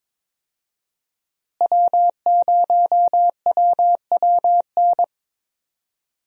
Morse code, heard as a keyed tone, about 22 wpm, 700 Hz. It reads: W0WWN
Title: USA Call Signs - 5 Characters - Repeated 3x - 22wpm